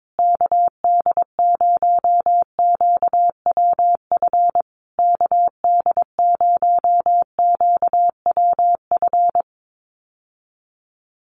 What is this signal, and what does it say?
Morse code, heard as a keyed tone, about 22 wpm, 700 Hz. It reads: KD0QWF KD0QWF
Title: USA Call Signs Repeated 3x 22wpm